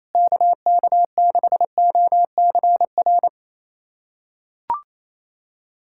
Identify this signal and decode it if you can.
Morse code, heard as a keyed tone, about 28 wpm, 700 Hz. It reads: KK6OCR
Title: USA Callsign DE Callsign 28wpm